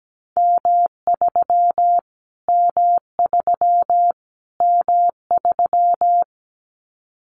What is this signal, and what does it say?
Morse code, heard as a keyed tone, about 17 wpm, 700 Hz. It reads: M3 M3 M3
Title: Callsign Prefixes for Top 25 DXCC - Repeated 3x - 17wpm